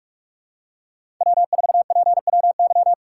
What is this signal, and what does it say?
Morse code, heard as a keyed tone, about 46 wpm, 700 Hz. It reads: W4PWY
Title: USA Call Signs - 5 Characters - 4x Character Spacing - 45wpm